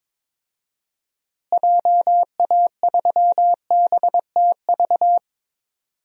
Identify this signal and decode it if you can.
Morse code, heard as a keyed tone, about 22 wpm, 700 Hz. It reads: JA3BTV
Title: DXCC-100 Call Signs - 6 Characters - Repeated 3x - 22wpm